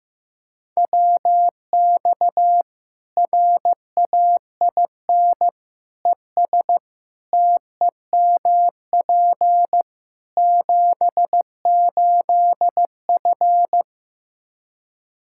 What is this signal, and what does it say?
Morse code, heard as a keyed tone, about 15 wpm, 700 Hz. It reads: WX RAIN ES TEMP 78F
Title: K1USN SST QSO Practice 15wpm